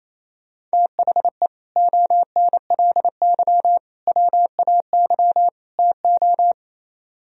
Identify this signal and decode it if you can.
Morse code, heard as a keyed tone, about 32 wpm, 700 Hz. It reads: THE ONLY WAY TO
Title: Speed-Racing - Sets of 4 Words 28wpm